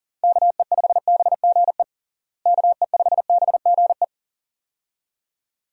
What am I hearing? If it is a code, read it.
Morse code, heard as a keyed tone, about 40 wpm, 700 Hz. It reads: KE5BGE KE5BGE
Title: USA Call Signs - 6 Characters - Repeated 2x - 40wpm